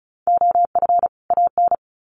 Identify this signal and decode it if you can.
Morse code, heard as a keyed tone, about 34 wpm, 700 Hz. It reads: OF AN
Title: Sets of 2 Words 35wpm